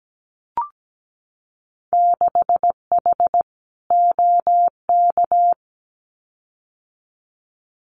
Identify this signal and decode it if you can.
Morse code, heard as a keyed tone, about 17 wpm, 700 Hz. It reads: 6H OK
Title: Winter Field Day Exchanges Encore 17wpm